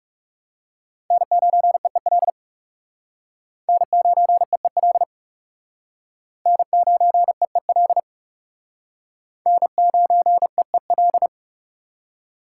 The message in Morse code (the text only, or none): N9EEL N9EEL N9EEL N9EEL